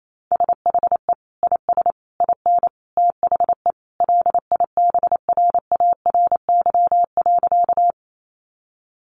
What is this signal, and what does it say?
Morse code, heard as a keyed tone, about 28 wpm, 700 Hz. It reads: SHE IS IN THE LIBRARY.
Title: Sentences from Top 600 Words 28wpm